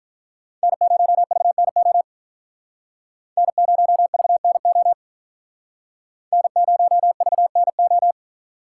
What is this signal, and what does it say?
Morse code, heard as a keyed tone, about 45 wpm, 700 Hz. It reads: N0VNO N0VNO N0VNO